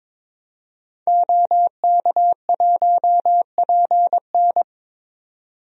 Morse code, heard as a keyed tone, about 22 wpm, 700 Hz. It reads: OK1PN
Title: DXCC-100 Call Signs - 5 Characters - 22wpm